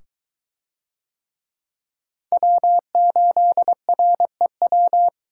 Morse code, heard as a keyed tone, about 22 wpm, 700 Hz. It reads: W8REW